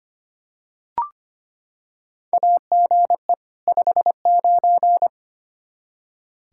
Morse code, heard as a keyed tone, about 25 wpm, 700 Hz. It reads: AGE 59
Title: Practice QSO - Exchange 3 - More Info - 25wpm